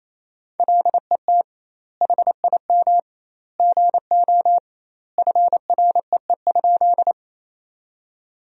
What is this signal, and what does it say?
Morse code, heard as a keyed tone, about 28 wpm, 700 Hz. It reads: LET HIM GO FREE?
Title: 2x Word-Spacing - Sentences from Top 500 Words - 28wpm